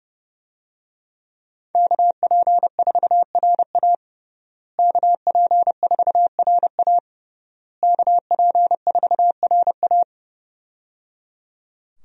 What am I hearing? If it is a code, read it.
Morse code, heard as a keyed tone, about 30 wpm, 700 Hz. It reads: KP4RA KP4RA KP4RA
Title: DXCC-100 Call Signs - 5 Characters - Repeated 3x - 30wpm